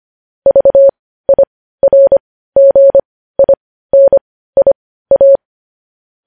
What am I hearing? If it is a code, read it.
Morse code, heard as a keyed tone, about 25 wpm, spaced as for 15 wpm, 550 Hz. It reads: VIRGINIA
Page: News Headlines in Morse Code at 15 WPM